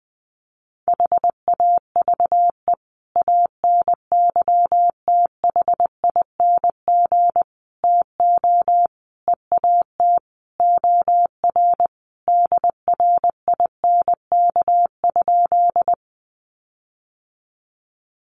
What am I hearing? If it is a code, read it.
Morse code, heard as a keyed tone, about 20 wpm, 700 Hz. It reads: HAVE ANYTHING TO EAT OR DRINK?
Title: Sentences from Top 1500 Words 20wpm